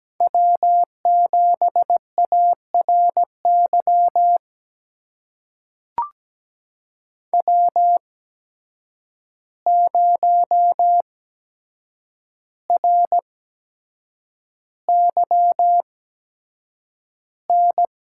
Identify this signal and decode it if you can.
Morse code, heard as a keyed tone, about 17 wpm, spaced as for 7 wpm, 700 Hz. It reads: W7ARY W0RYN
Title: USA Call Signs - 5 Characters - 8x Character Spacing - 17wpm